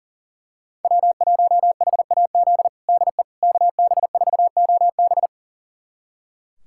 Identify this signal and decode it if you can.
Morse code, heard as a keyed tone, about 40 wpm, 700 Hz. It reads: W1HAZ DE KB4OB
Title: USA Callsign DE Callsign - 5 Characters - 40wpm